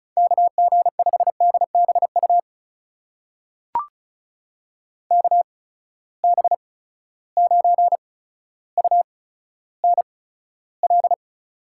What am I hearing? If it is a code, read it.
Morse code, heard as a keyed tone, about 36 wpm, spaced as for 15 wpm, 700 Hz. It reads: KG5DBU KB9UNL
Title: USA Call Signs - 6 Characters - 8x Character Spacing - 35wpm